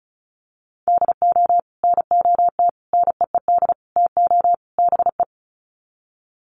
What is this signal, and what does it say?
Morse code, heard as a keyed tone, about 34 wpm, 700 Hz. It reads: DO NOT NEED TO BE
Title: Sets of 5 Words Encore 35wpm